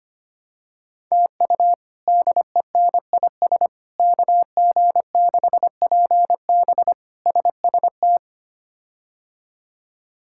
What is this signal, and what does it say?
Morse code, heard as a keyed tone, about 25 wpm, 700 Hz. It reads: TU DENIS KG6PB SST
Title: K1USN SST QSO Practice - 25wpm